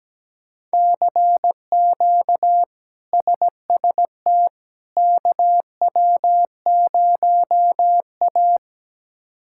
Morse code, heard as a keyed tone, about 17 wpm, 700 Hz. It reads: CQ SST KW0A
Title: K1USN SST QSO Practice - 17wpm